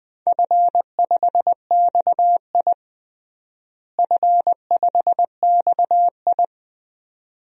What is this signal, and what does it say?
Morse code, heard as a keyed tone, about 20 wpm, 700 Hz. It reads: F5XI F5XI
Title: DXCC-100 Call Signs - 4 Characters - Repeated 2x - 20wpm